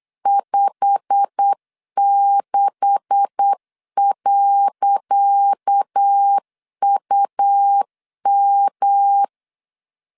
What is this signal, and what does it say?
Morse code, heard as a keyed tone, about 8 wpm, 800 Hz. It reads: 56.UM
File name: LEZIONI / 10 SIGNS / 10C DOT 8wpm 3649